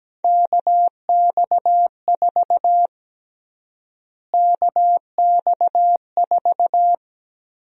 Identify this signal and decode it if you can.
Morse code, heard as a keyed tone, about 17 wpm, 700 Hz. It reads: KX4 KX4
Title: USA Call Sign Prefixes - Repeated 2x - 17wpm